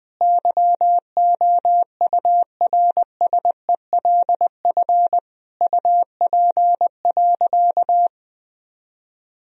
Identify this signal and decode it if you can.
Morse code, heard as a keyed tone, about 20 wpm, 700 Hz. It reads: YOURSELF UP.
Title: Sentences from Top 1000 Words - Review Part 1 - 20wpm